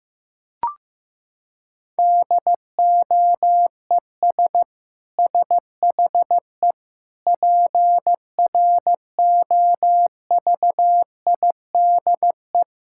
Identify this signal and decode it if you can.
Morse code, heard as a keyed tone, about 16 wpm, 700 Hz. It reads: DOES SHE PROVIDE
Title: Sentences from Top 1000 Words - Review Part 6 - 15wpm